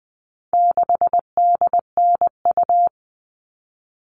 Morse code, heard as a keyed tone, about 20 wpm, 700 Hz. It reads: 6DNU
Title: All Cut Numbers - 4-Digit 20wpm